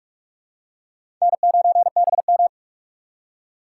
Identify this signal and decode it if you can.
Morse code, heard as a keyed tone, about 46 wpm, 700 Hz. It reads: N9BM
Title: USA Call Signs - 4 Characters - 6x Character Spacing - 45wpm